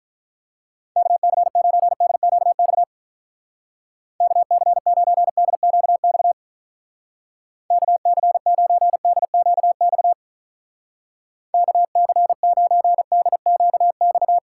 Morse code, heard as a keyed tone, about 45 wpm, 700 Hz. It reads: KC9DQX KC9DQX KC9DQX KC9DQX